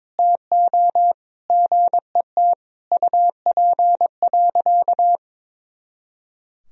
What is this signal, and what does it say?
Morse code, heard as a keyed tone, about 22 wpm, 700 Hz. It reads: TO GET UP.
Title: Sentences from Top 500 Words and Abbreviations 22wpm